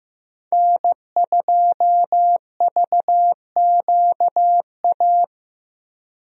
Morse code, heard as a keyed tone, about 15 wpm, 700 Hz. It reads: N2VQA
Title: USA Call Signs - 5 Characters - Repeated 2x - 15wpm